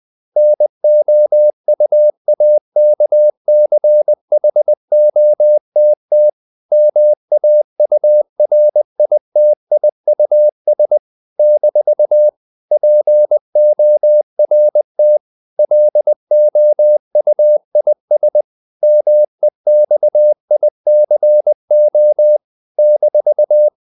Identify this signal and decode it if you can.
Morse code, heard as a keyed tone, about 20 wpm, 600 Hz. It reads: NOUAKCHOTT MAURITIUS - PORT LOUIS MEXICO -
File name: CW / Practice / NationalCapitals20